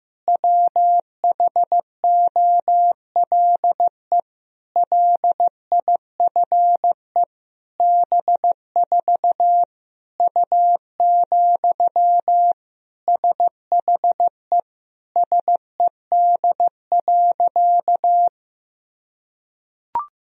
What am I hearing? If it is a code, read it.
Morse code, heard as a keyed tone, about 15 wpm, 700 Hz. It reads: WHOLE LIFE B4 U, SHE SED.
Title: Sentences from Top 400 Words and Abbreviations 15wpm